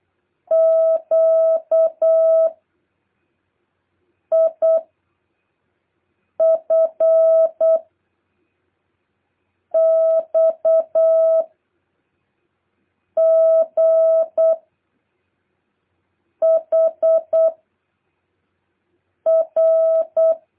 Morse code, heard as a keyed tone, about 8 wpm, 650 Hz. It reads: Q I F X G H R
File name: jh8cbh / 2023cw